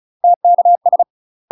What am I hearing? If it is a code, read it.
Morse code, heard as a keyed tone, about 36 wpm, 700 Hz. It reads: TKS
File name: UK MASTER QSOs TRAINING/35WPM